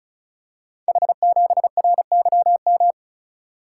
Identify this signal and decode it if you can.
Morse code, heard as a keyed tone, about 36 wpm, 700 Hz. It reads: H7RYM
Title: Code Groups - 5 Letter:Numbers - 35wpm